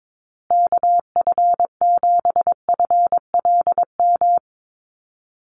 Morse code, heard as a keyed tone, about 22 wpm, 700 Hz. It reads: KF7FLM
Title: USA Call Signs Repeated 3x 22wpm